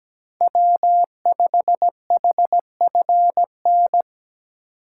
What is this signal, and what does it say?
Morse code, heard as a keyed tone, about 17 wpm, 700 Hz. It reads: W5HFN
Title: USA Call Signs - 5 Characters - 2x Character Spacing - 17wpm